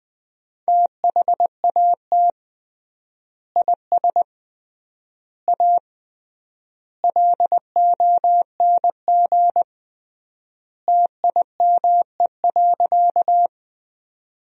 Morse code, heard as a keyed tone, about 20 wpm, 700 Hz. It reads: THAT IS A LONG TIME.
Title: 3x Word-Spacing - Sentences from Top 100 Words - 20wpm